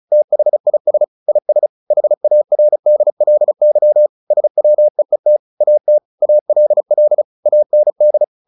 Morse code, heard as a keyed tone, about 34 wpm, 600 Hz. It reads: THIS IS HARDLY SWEET AT ALL AND
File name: the old man and the sea / the old man and the sea 1 35wpm